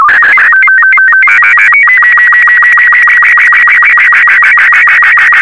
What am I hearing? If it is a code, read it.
SSTV (slow-scan television), Robot 36.